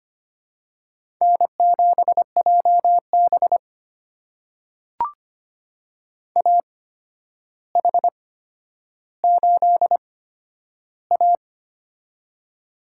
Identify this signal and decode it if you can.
Morse code, heard as a keyed tone, about 25 wpm, spaced as for 11 wpm, 700 Hz. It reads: N7JB AH8A
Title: USA Call Signs - 4 Characters - 8x Character Spacing - 25wpm